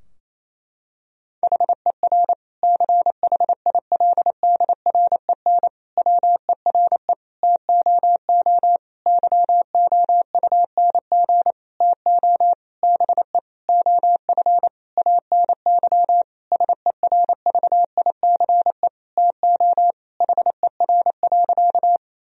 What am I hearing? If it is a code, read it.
Morse code, heard as a keyed tone, about 28 wpm, 700 Hz. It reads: HER CHILDREN WERE TOO YOUNG TO BE OF ANY SERVICE TO HER.
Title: Sentences from Top 1000 Words - Review Part 4 - 28wpm